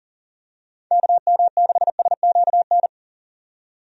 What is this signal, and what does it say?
Morse code, heard as a keyed tone, about 40 wpm, 700 Hz. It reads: KM6SQN
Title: USA Call Signs - 6 Characters - 40wpm